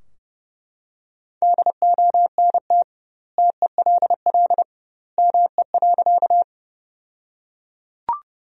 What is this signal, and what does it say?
Morse code, heard as a keyed tone, about 32 wpm, 700 Hz. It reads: DONT TELL ME.
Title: 2x Word-Spacing - Sentences from Top 300 Words - 30wpm